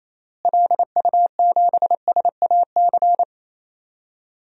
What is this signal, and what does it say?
Morse code, heard as a keyed tone, about 28 wpm, 700 Hz. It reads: LU7SAC
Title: DXCC-100 Call Signs - 6 Characters - Repeated 3x - 28wpm